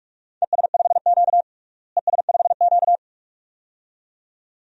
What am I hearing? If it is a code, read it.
Morse code, heard as a keyed tone, about 46 wpm, 700 Hz. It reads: ES5Q ES5Q
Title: DXCC-100 Call Signs - 4 Characters - Repeated 2x - 45wpm